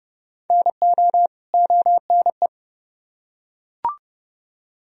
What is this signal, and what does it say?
Morse code, heard as a keyed tone, about 29 wpm, 700 Hz. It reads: NO ONE